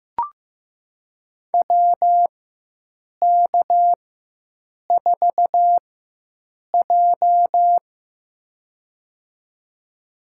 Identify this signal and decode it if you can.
Morse code, heard as a keyed tone, about 15 wpm, spaced as for 7 wpm, 700 Hz. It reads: WK4J
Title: USA Call Signs - 4x Character Spacing - 15wpm